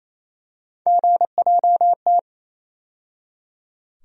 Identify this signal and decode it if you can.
Morse code, heard as a keyed tone, about 28 wpm, 700 Hz. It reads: GJT
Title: USA Call Sign Postfixes - 3 Letters 28wpm